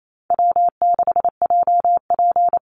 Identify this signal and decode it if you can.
Morse code, heard as a keyed tone, about 28 wpm, 700 Hz. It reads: W6JP
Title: NA SOTA Chaser Callsigns 28wpm